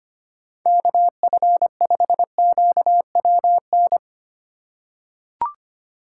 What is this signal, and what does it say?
Morse code, heard as a keyed tone, about 25 wpm, 700 Hz. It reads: KF5QWN
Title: USA Call Signs - 6 Characters - 3x Character Spacing - 25wpm